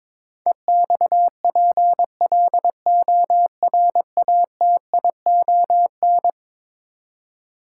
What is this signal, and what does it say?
Morse code, heard as a keyed tone, about 22 wpm, 700 Hz. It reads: EXPLORATION
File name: practice / 22wpm / Eleven Letter Words - 22wpm